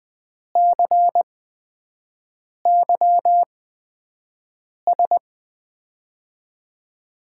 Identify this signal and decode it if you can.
Morse code, heard as a keyed tone, about 20 wpm, spaced as for 5 wpm, 700 Hz. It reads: CYS
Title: USA Call Signs - 6 Characters - 8x Character Spacing - 20wpm